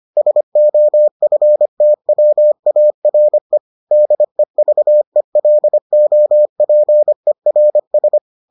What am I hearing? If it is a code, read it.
Morse code, heard as a keyed tone, about 25 wpm, 600 Hz. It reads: SOFTWARE DEVELOPERS